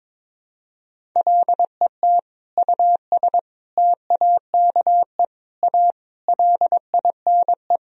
Morse code, heard as a keyed tone, about 22 wpm, 700 Hz. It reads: LET US TAKE A LINE
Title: Sentences from Top 1000 Words - Review Part 1 - 22wpm